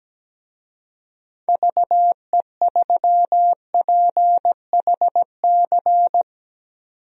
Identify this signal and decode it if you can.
Morse code, heard as a keyed tone, about 17 wpm, 700 Hz. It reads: VE3PHC
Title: DXCC-100 Call Signs - 6 Characters - 17wpm